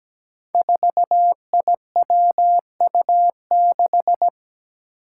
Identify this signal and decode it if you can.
Morse code, heard as a keyed tone, about 17 wpm, 700 Hz. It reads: 4IWU6
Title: Five Challenging Letter-Number Combinations 17wpm